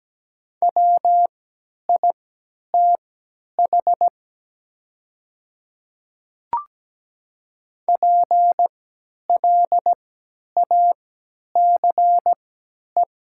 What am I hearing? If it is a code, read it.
Morse code, heard as a keyed tone, about 17 wpm, spaced as for 10 wpm, 700 Hz. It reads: WITH PLACE
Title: 3x Character Spacing - Top 200 Words - 17wpm